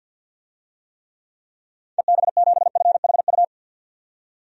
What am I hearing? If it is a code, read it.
Morse code, heard as a keyed tone, about 48 wpm, 700 Hz. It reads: EB7FHU
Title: DXCC-100 Call Signs - 6 Characters - 50wpm